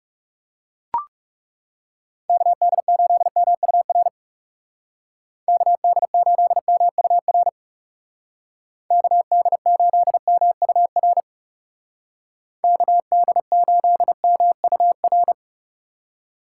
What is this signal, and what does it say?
Morse code, heard as a keyed tone, about 37 wpm, 700 Hz. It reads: KD8MUR KD8MUR KD8MUR KD8MUR
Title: Speed Racing - USA Call Signs 30wpm